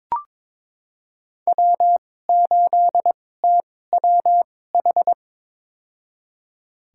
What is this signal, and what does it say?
Morse code, heard as a keyed tone, about 22 wpm, spaced as for 16 wpm, 700 Hz. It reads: W8TWH
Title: USA Call Signs - 2x Character Spacing - 22wpm